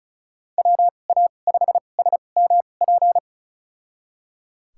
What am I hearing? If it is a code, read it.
Morse code, heard as a keyed tone, about 36 wpm, spaced as for 25 wpm, 700 Hz. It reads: WA5SMP
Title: USA Call Signs - 2x Character Spacing - 35wpm